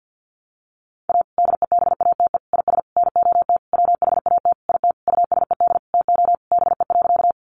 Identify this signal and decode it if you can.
Morse code, heard as a keyed tone, about 50 wpm, 700 Hz. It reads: A DEBATE IS NOT WHAT IT USED TO BE.
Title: Sentences from Top 1100 Words 50wpm